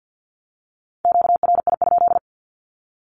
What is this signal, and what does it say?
Morse code, heard as a keyed tone, about 48 wpm, 700 Hz. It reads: QRI?
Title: Q-Codes 50wpm